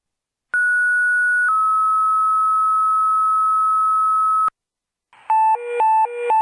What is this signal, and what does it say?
Two-tone page, heard as a one-second tone, then a three-second tone, about 1460 then 1290 Hz, with noise. It is heard from 0.5 to 4.5 s.